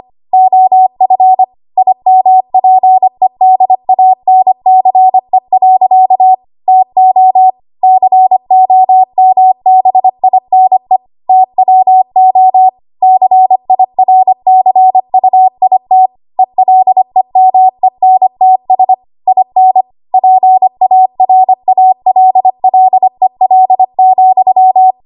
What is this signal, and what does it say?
Morse code, heard as a keyed tone, about 25 wpm, 750 Hz. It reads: OF IMPEDANCE. TO COMBINE TWO CIRCUIT ELEMENTS IN PARALLEL,
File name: Archive / 25 WPM / 25WPM